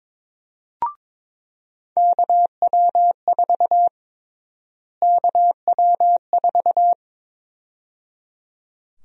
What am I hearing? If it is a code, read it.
Morse code, heard as a keyed tone, about 22 wpm, 700 Hz. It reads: KW4 KW4
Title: USA Call Sign Prefixes - Repeated 2x - 22wpm